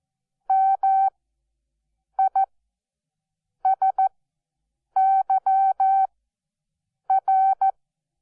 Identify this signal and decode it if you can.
Morse code, heard as a keyed tone, about 14 wpm, spaced as for 6 wpm, 770 Hz. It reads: MISYR
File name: Onlinelehrgang / morsen / DJ4UF Morsen12